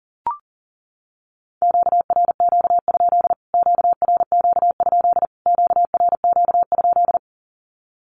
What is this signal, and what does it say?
Morse code, heard as a keyed tone, about 40 wpm, 700 Hz. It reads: QRQ? QRQ? QRQ?
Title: Q-Codes Repeated 3x 40wpm